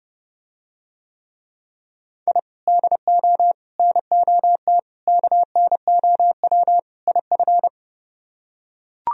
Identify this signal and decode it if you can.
Morse code, heard as a keyed tone, about 31 wpm, 700 Hz. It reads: I DO NOT KNOW IF